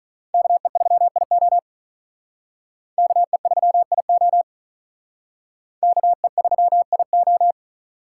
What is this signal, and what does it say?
Morse code, heard as a keyed tone, about 39 wpm, 700 Hz. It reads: KE3IO KE3IO KE3IO
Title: Speed Racing - USA Call Signs 35wpm